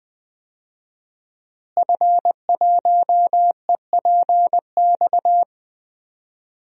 Morse code, heard as a keyed tone, about 20 wpm, 700 Hz. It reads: F1EPX
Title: DXCC-100 Call Signs - 5 Characters - Repeated 2x - 20wpm